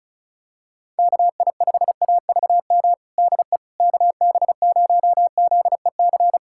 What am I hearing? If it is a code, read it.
Morse code, heard as a keyed tone, about 36 wpm, 700 Hz. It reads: KI5AVM DE KB0ZEC
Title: USA Callsign DE Callsign - 6 Characters - 35wpm